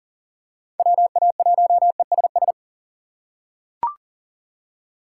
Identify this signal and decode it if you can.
Morse code, heard as a keyed tone, about 40 wpm, 700 Hz. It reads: WA1ESS
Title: USA Call Signs - 6 Characters - 2x Character Spacing - 40wpm